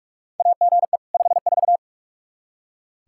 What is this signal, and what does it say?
Morse code, heard as a keyed tone, about 41 wpm, 700 Hz. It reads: AGE 54
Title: K1USN SST QSO Practice 45wpm